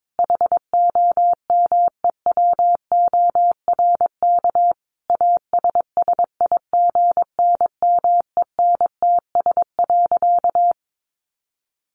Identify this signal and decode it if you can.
Morse code, heard as a keyed tone, about 22 wpm, 700 Hz. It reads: HOMEWORK ASSIGNMENTS.
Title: Easy Phrases 22wpm